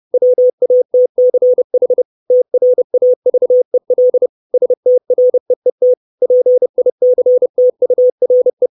Morse code, heard as a keyed tone, about 30 wpm, 500 Hz. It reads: WATCH TRAVEL STREET PICTURE
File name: 632 Words 30WPM